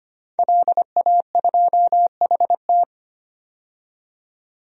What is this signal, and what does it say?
Morse code, heard as a keyed tone, about 25 wpm, 700 Hz. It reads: LA2HT
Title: DXCC-100 Call Signs - 5 Characters - 25wpm